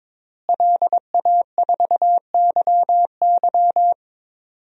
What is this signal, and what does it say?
Morse code, heard as a keyed tone, about 22 wpm, 700 Hz. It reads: LA4YY
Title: DXCC-100 Call Signs - 5 Characters - Repeated 2x - 22wpm